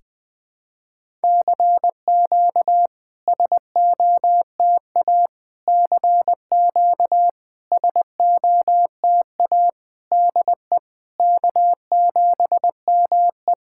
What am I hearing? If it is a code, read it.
Morse code, heard as a keyed tone, about 21 wpm, 700 Hz. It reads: CQ SOTA CQ SOTA DE K7ME
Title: SOTA QSO Practice 20wpm